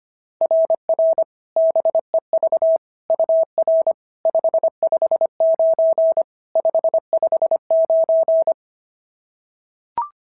Morse code, heard as a keyed tone, about 25 wpm, 660 Hz. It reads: RR BEV UR 559 559
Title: SOTA QSO Practice 25wpm